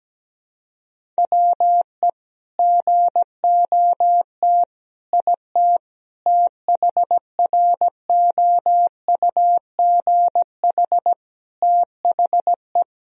Morse code, heard as a keyed tone, about 17 wpm, 700 Hz. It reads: WE GOT IT THROUGH THE
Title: Sentences from Top 1000 Words - Review Part 1 - 17wpm